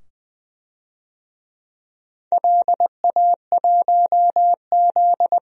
Morse code, heard as a keyed tone, about 20 wpm, 700 Hz. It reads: LA1Z
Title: DXCC-100 Call Signs - 4 Characters - 20wpm